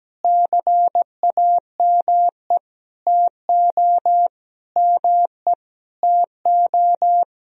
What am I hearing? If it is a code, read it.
Morse code, heard as a keyed tone, about 17 wpm, 700 Hz. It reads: CAME TO ME TO
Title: Sentences from Top 200 Words and Abbreviations 17wpm